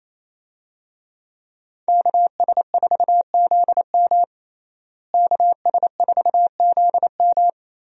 Morse code, heard as a keyed tone, about 28 wpm, 700 Hz. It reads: KS4ZM KS4ZM
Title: USA Call Signs - 5 Characters - Repeated 2x - 28wpm